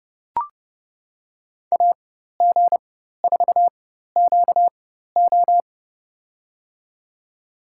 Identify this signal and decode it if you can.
Morse code, heard as a keyed tone, about 29 wpm, spaced as for 14 wpm, 700 Hz. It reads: AG4QO
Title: USA Call Signs - 5 Characters - 4x Character Spacing - 30wpm